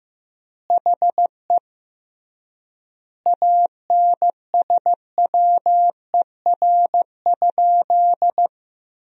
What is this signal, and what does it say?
Morse code, heard as a keyed tone, about 14 wpm, 700 Hz. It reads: HE ANSWER?
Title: 3x Word-Spacing - Sentences from Top 300 Words - 15wpm